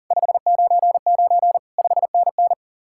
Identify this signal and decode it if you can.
Morse code, heard as a keyed tone, about 50 wpm, 700 Hz. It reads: 599 5NN
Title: Speed-Racing Common QSO Phrases - 40wpm